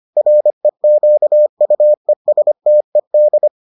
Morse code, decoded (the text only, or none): REQUESTED